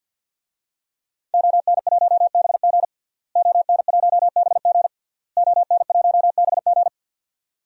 Morse code, heard as a keyed tone, about 51 wpm, 700 Hz. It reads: ON1BG ON1BG ON1BG
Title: DXCC-100 Call Signs - 5 Characters - Repeated 3x - 50wpm